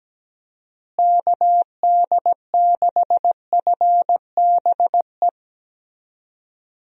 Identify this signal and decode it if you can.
Morse code, heard as a keyed tone, about 17 wpm, 700 Hz. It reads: KD6FBE
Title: USA Call Signs - 6 Characters - Repeated 2x - 17wpm